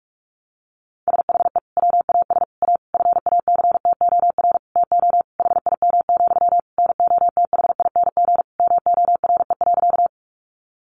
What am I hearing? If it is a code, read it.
Morse code, heard as a keyed tone, about 44 wpm, 700 Hz. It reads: SHE WAS A FACTOR TO HIM, NOTHING MORE.
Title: Sentences from Top 1300 Words 45wpm